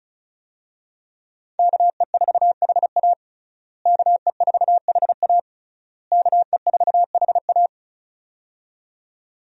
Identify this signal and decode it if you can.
Morse code, heard as a keyed tone, about 36 wpm, 700 Hz. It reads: KE4HA KE4HA KE4HA
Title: USA Call Signs - 5 Characters - Repeated 3x - 35wpm